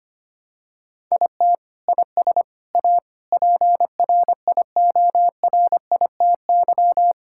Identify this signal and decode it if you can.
Morse code, heard as a keyed tone, about 25 wpm, 700 Hz. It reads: IT IS A PRIORITY
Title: Sentences from Top 1600 Words 25wpm